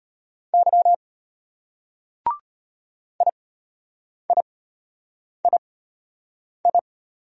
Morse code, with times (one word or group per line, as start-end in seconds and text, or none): 0.5-0.9 Y
3.2-6.8 IIII